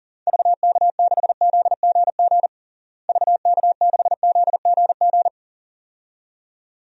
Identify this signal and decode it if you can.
Morse code, heard as a keyed tone, about 40 wpm, 700 Hz. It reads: VK6ZGG VK6ZGG
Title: DXCC-100 Call Signs - 6 Characters - Repeated 2x - 40wpm